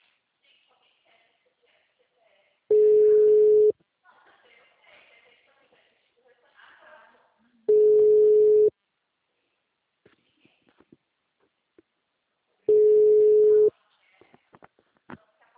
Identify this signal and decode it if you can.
Ringback tone (Europe, 425 Hz), with speech.
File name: out- -2004- - - .97